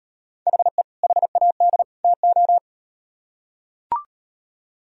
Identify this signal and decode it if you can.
Morse code, heard as a keyed tone, about 38 wpm, 700 Hz. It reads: HE HAD TO